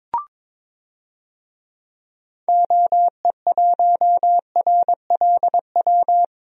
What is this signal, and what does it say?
Morse code, heard as a keyed tone, about 22 wpm, 700 Hz. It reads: OE1RLW
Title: DXCC-100 Call Signs - 6 Characters - Repeated 3x - 22wpm